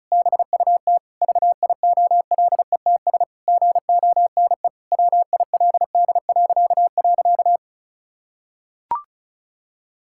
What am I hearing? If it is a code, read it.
Morse code, heard as a keyed tone, about 36 wpm, 700 Hz. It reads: BUT VIOLETS GONE WILD..
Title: Pangram Challenge v2 - 35wpm